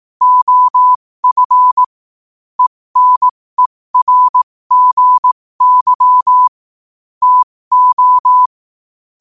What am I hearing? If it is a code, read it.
Morse code, heard as a keyed tone, about 18 wpm, spaced as for 15 wpm, 1000 Hz. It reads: OF ENERGY TO
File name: qotdfiles / 15wpm